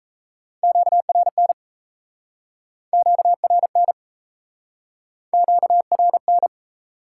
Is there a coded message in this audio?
Yes: Morse code from 0.6 to 6.5 s, heard as a keyed tone, about 38 wpm, 700 Hz. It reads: QRN QRN QRN